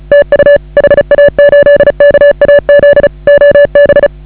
Morse code, heard as a keyed tone, about 34 wpm, 560 Hz. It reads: TU HA8KAZ OB
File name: AUDIO2019 / HA8KAZ CW